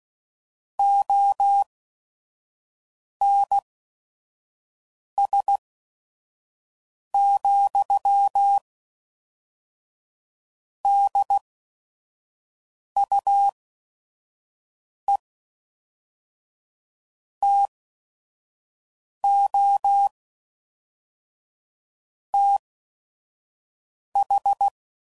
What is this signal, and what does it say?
Morse code, heard as a keyed tone, about 16 wpm, spaced as for 5 wpm, 800 Hz. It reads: ONS, DUE TO TH
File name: CW Code Groups @5WPM / Technical Subjects / AND Gate